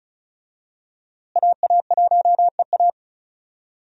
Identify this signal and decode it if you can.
Morse code, heard as a keyed tone, about 36 wpm, 700 Hz. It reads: AA1EA
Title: USA Call Signs - 5 Characters - Repeated 3x - 35wpm